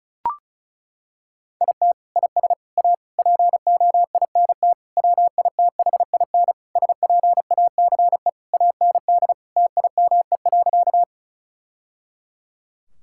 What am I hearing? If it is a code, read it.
Morse code, heard as a keyed tone, about 36 wpm, 700 Hz. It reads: IT IS A POINT WITHIN SPACE AND TIME.
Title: Sentences from Top 1000 Words - Review Part 5 - 35wpm